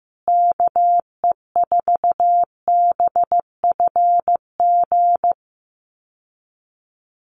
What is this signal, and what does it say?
Morse code, heard as a keyed tone, about 15 wpm, 700 Hz. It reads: KE4BFG
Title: USA Call Signs Repeated 3x 15wpm